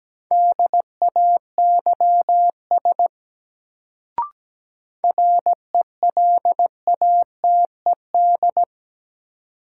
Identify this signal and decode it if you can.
Morse code, heard as a keyed tone, about 17 wpm, 700 Hz. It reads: DAYS RELATED
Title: Sentences from Top 1000 Words - Review Part 5 - New Word Drill - 17wpm